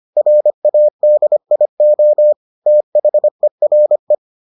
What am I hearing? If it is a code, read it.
Morse code, heard as a keyed tone, about 25 wpm, 600 Hz. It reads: RADIO THERE